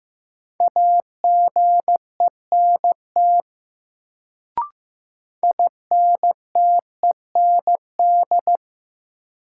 Morse code, heard as a keyed tone, about 15 wpm, 700 Hz. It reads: AGENT INTEND
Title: Sentences from Top 1600 Words - New Word Drill Alt - 15wpm